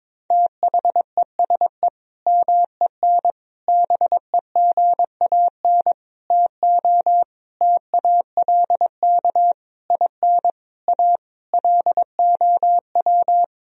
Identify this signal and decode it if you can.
Morse code, heard as a keyed tone, about 21 wpm, 700 Hz. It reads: THESE MEN BEGAN TO TALK IN A LOW
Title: Sentences from Top 1000 Words - Review Part 1 - 22wpm